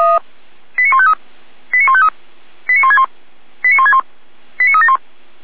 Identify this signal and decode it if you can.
Five-tone selective call (ZVEI), heard as quick runs of five tones, tones of about 70 ms. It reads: 89153 89153 89161 89161 89371
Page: Server at audio.f85360.feuerwehr-orlamuende.org